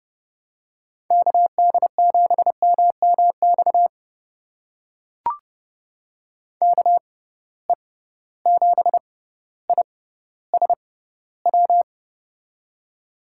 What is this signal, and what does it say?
Morse code, heard as a keyed tone, about 30 wpm, spaced as for 15 wpm, 700 Hz. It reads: KD7MMX KE7ISW
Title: USA Call Signs - 6 Characters - 6x Character Spacing - 30wpm